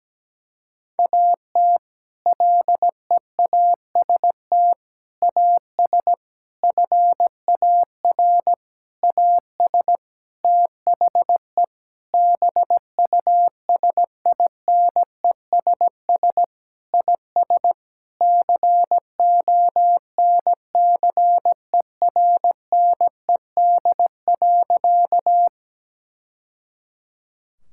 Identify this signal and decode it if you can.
Morse code, heard as a keyed tone, about 17 wpm, 700 Hz. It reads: AT LEAST AS FAR AS THE BUSINESS IS CONCERNED.
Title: Sentences from Top 1000 Words - Review Part 6 - 17wpm